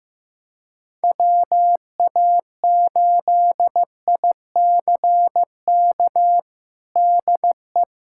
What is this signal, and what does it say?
Morse code, heard as a keyed tone, about 15 wpm, 700 Hz. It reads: WA8ICK DE
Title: USA Callsign DE Callsign 15wpm